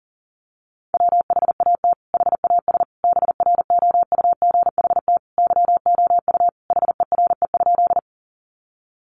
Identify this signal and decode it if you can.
Morse code, heard as a keyed tone, about 40 wpm, 700 Hz. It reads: WHAT HAS BROUGHT YOU HERE?
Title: Sentences from Top 500 Words 40wpm